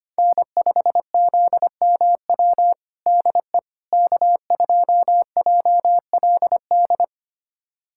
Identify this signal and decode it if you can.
Morse code, heard as a keyed tone, about 25 wpm, 700 Hz. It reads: N5ZMW DE K2JLD
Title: USA Callsign DE Callsign - 5 Characters - 25wpm